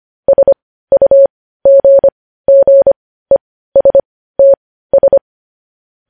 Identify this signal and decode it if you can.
Morse code, heard as a keyed tone, about 25 wpm, spaced as for 15 wpm, 550 Hz. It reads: SUGGESTS
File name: podcast / News headlines-1x-25x15